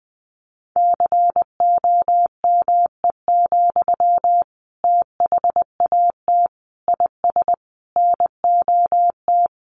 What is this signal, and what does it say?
Morse code, heard as a keyed tone, about 20 wpm, 700 Hz. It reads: COME, THAT IS NOT
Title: Sentences from Top 700 Words 20wpm